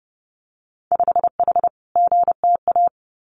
Morse code, heard as a keyed tone, about 30 wpm, 700 Hz. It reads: 5H GTA
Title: Winter Field Day Exchanges Encore 30wpm